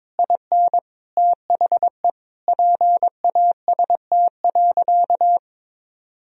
Morse code, heard as a keyed tone, about 22 wpm, 700 Hz. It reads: IN THE PAST.
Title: Sentences from Top 1000 Words - Review Part 4 - 22wpm